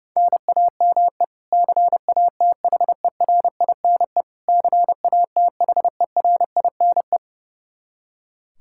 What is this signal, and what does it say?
Morse code, heard as a keyed tone, about 32 wpm, 700 Hz. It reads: NAME CATHERINE CATHERINE
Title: Practice QSO - Exchange 1 - Beginning - 30wpm